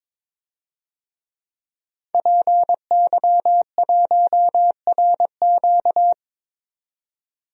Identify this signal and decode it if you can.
Morse code, heard as a keyed tone, about 22 wpm, 700 Hz. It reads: PY1RQ
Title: DXCC-100 Call Signs - 5 Characters - 22wpm